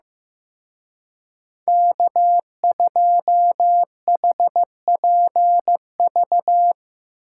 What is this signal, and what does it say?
Morse code, heard as a keyed tone, about 17 wpm, 700 Hz. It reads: K2HPV
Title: Speed Racing - USA 1x3 Call Signs 15wpm